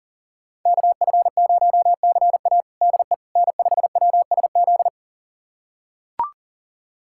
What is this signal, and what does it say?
Morse code, heard as a keyed tone, about 37 wpm, 700 Hz. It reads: KF0CA DE N5WSZ
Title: USA Callsign DE Callsign - 5 Characters - 40wpm